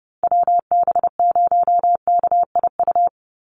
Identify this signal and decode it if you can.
Morse code, heard as a keyed tone, about 30 wpm, 700 Hz. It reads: WB0KIU
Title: NA SOTA Chaser Callsigns 30wpm